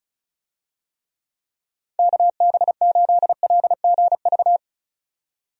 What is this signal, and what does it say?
Morse code, heard as a keyed tone, about 34 wpm, 700 Hz. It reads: KB8LGV
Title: USA Call Signs - 6 Characters - 35wpm